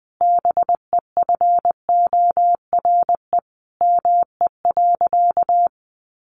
Morse code, heard as a keyed tone, about 20 wpm, 700 Hz. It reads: BEFORE ME.
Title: Sentences from Top 1300 Words 20wpm